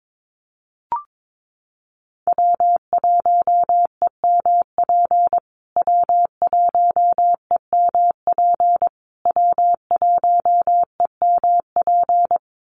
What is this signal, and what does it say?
Morse code, heard as a keyed tone, about 22 wpm, 700 Hz. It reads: W1EMP W1EMP W1EMP
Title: USA Call Signs Repeated 3x 22wpm